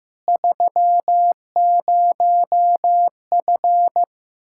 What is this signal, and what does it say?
Morse code, heard as a keyed tone, about 15 wpm, 700 Hz. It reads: 30F
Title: Practice QSO - Exchange 2 - Info - 15wpm